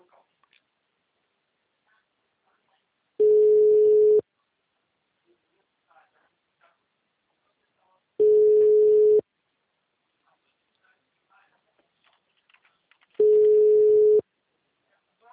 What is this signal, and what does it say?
Ringback tone (Europe, 425 Hz), with speech.